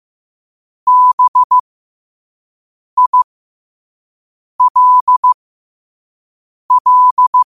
Morse code, heard as a keyed tone, about 15 wpm, spaced as for 5 wpm, 1000 Hz. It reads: BILL
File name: qotdfiles / 5wpm